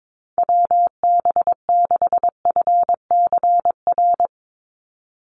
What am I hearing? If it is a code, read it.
Morse code, heard as a keyed tone, about 22 wpm, 700 Hz. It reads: WB6FCR
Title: USA Call Signs 22wpm